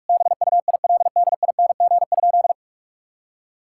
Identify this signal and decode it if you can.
Morse code, heard as a keyed tone, about 46 wpm, 700 Hz. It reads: BUILDING?
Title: Sentences from Top 1000 Words - Review Part 6 - 45wpm